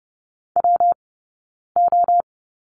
Morse code, heard as a keyed tone, about 30 wpm, spaced as for 9 wpm, 690 Hz. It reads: WO
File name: Lesson 19d - Letter w - Two Characters 30wpm